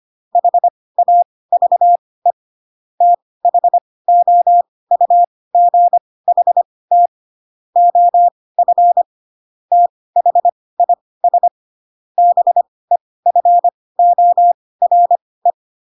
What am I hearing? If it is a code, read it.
Morse code, heard as a keyed tone, about 25 wpm, spaced as for 18 wpm, 700 Hz. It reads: HAVE THOUGHT OF THIS BEFORE
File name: Pinocchio ch3 12 25 ik1hge